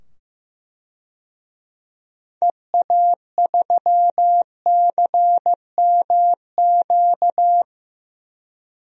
Morse code, heard as a keyed tone, about 15 wpm, 690 Hz. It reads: EA3CMQ
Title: DXCC-100 Call Signs - 6 Characters - 15wpm